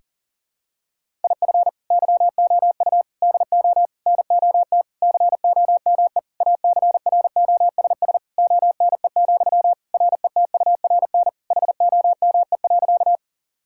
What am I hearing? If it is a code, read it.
Morse code, heard as a keyed tone, about 40 wpm, 690 Hz. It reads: IF YOU DO NOT COME ACROSS ONE, RETURN HOME.
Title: Sentences from Top 1000 Words - Review Part 5 - 40wpm